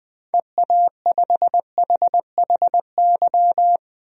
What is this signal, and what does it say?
Morse code, heard as a keyed tone, about 21 wpm, 700 Hz. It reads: EA5HHY
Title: DXCC-100 Call Signs - 6 Characters - 20wpm